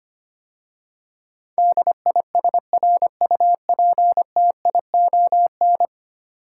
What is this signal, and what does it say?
Morse code, heard as a keyed tone, about 25 wpm, 700 Hz. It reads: DISRUPTION
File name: 10 Letter Words - First 3 Letters Given - 25wpm